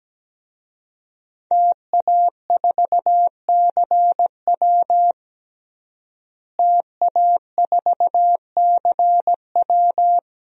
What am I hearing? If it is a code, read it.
Morse code, heard as a keyed tone, about 17 wpm, 700 Hz. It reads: TA4CW TA4CW
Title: DXCC-100 Call Signs - 5 Characters - Repeated 2x - 17wpm